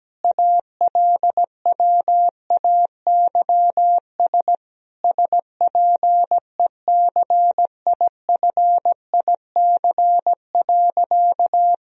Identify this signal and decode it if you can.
Morse code, heard as a keyed tone, about 17 wpm, 700 Hz. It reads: ALWAYS SPECIFIC.
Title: Sentences from Top 1000 Words - Review Part 3 - 17wpm